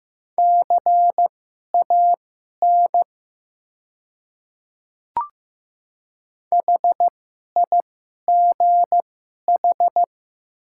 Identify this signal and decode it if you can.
Morse code, heard as a keyed tone, about 15 wpm, spaced as for 11 wpm, 700 Hz. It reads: CAN HIGH
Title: 2x Character Spacing - Top 200 Words - 15wpm